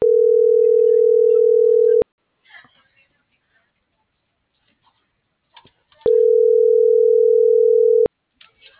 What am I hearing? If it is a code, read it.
Ringback tone (North America), 440 + 480 Hz.